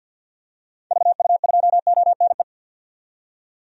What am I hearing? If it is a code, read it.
Morse code, heard as a keyed tone, about 46 wpm, 700 Hz. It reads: VU2ONE
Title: DXCC-100 Call Signs - 6 Characters - 50wpm